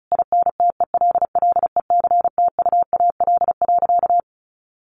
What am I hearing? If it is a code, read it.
Morse code, heard as a keyed tone, about 34 wpm, 700 Hz. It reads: INTELLECTUAL.
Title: A Sound Of Thunder 35wpm